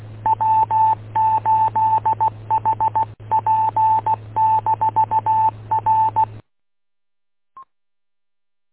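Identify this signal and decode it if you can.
Morse code, heard as a keyed tone, about 16 wpm, 880 Hz, with radio noise. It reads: W8HP-R
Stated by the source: Morse code recorded by W8UM on 145.230.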